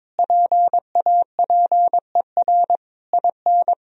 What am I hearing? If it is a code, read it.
Morse code, heard as a keyed tone, about 22 wpm, 700 Hz. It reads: PAPER IN
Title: Sentences from Top 1000 Words - Review Part 2 - 22wpm